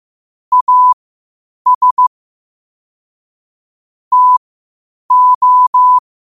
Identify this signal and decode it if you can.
Morse code, heard as a keyed tone, about 15 wpm, spaced as for 8 wpm, 1000 Hz. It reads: AS TO
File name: qotdfiles / 8wpm